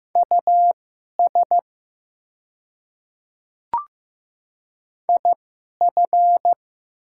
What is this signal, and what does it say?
Morse code, heard as a keyed tone, about 15 wpm, spaced as for 11 wpm, 700 Hz. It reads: US IF
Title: 2x Character Spacing - Top 500 Words - 15wpm